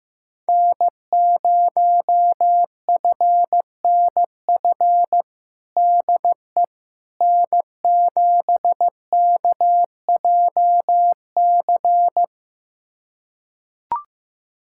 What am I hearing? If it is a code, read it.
Morse code, heard as a keyed tone, about 15 wpm, 700 Hz. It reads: N0FNF DE N7KJC
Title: USA Callsign DE Callsign - 5 Characters - 15wpm